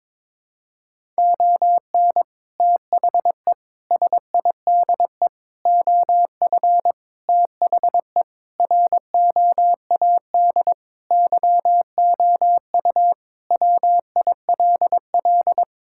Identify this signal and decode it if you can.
Morse code, heard as a keyed tone, about 22 wpm, 700 Hz. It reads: ON THE SIDE OF THE ROAD YOU WILL
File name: practice / 22wpm / Sentences with 3-Word Phrases - 22wpm